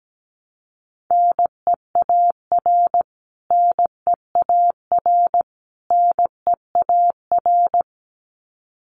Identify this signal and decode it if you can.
Morse code, heard as a keyed tone, about 17 wpm, 700 Hz. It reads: NEAR NEAR NEAR
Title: Top 500 Words Repeated 17wpm